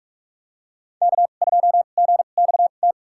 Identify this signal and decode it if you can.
Morse code, heard as a keyed tone, about 47 wpm, spaced as for 33 wpm, 700 Hz. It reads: K2GXT
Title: USA Call Signs - 2x Character Spacing - 45wpm